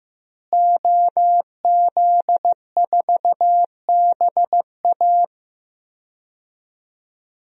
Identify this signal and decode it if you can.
Morse code, heard as a keyed tone, about 16 wpm, 700 Hz. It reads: OZ4BA
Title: DXCC-100 Call Signs - 5 Characters - Repeated 3x - 15wpm